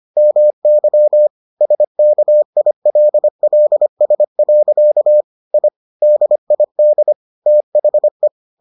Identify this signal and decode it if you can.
Morse code, heard as a keyed tone, about 25 wpm, 600 Hz. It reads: MY SKILLS. I DID THE